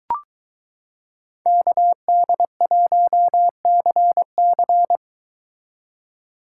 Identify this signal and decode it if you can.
Morse code, heard as a keyed tone, about 22 wpm, 700 Hz. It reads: KD1CC